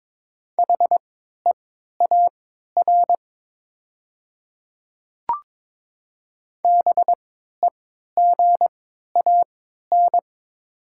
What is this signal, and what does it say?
Morse code, heard as a keyed tone, about 22 wpm, spaced as for 13 wpm, 700 Hz. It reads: HEAR BEGAN
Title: 3x Character Spacing - Top 300 Words - 22wpm